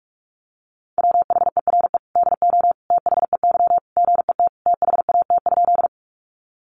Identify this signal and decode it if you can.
Morse code, heard as a keyed tone, about 44 wpm, 700 Hz. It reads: WHERE DO THEY GET THAT?
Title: Sentences from Top 200 Words 45wpm